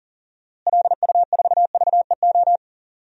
Morse code, heard as a keyed tone, about 37 wpm, 700 Hz. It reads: LU4VEO
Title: DXCC-100 Call Signs - 6 Characters - Repeated 2x - 40wpm